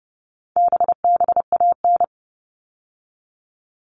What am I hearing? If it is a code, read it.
Morse code, heard as a keyed tone, about 30 wpm, 700 Hz. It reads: BBAN
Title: All Cut Numbers - 4-Digit 30wpm